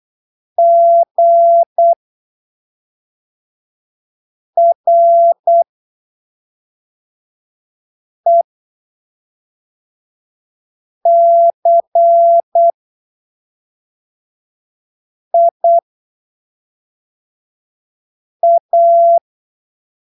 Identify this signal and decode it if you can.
Morse code, heard as a keyed tone, about 8 wpm, spaced as for 3 wpm, 680 Hz. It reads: GRECIA
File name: arquivos / texto / texto 08 08ppm